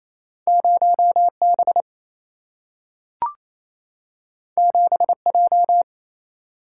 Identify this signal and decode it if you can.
Morse code, heard as a keyed tone, about 28 wpm, 700 Hz. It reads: 0B 7J
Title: Single-Letter Single-Number 28wpm